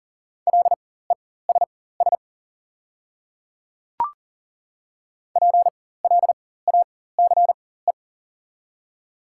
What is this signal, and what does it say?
Morse code, heard as a keyed tone, about 40 wpm, spaced as for 19 wpm, 700 Hz. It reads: LESS PLACE
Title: 4x Character Spacing - Top 500 Words - 40wpm